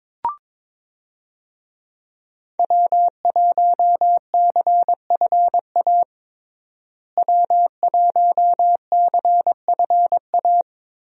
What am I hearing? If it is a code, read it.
Morse code, heard as a keyed tone, about 21 wpm, 700 Hz. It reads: W1CFA W1CFA
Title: USA Call Signs - 5 Characters - Repeated 3x - 22wpm